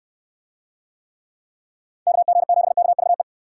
Morse code, heard as a keyed tone, about 69 wpm, 700 Hz. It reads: KG7KVE